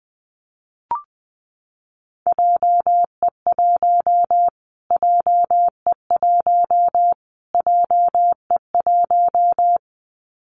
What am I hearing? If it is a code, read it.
Morse code, heard as a keyed tone, about 20 wpm, 700 Hz. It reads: JE1 JE1 JE1
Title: Callsign Prefixes for Top 25 DXCC - Repeated 3x - 20wpm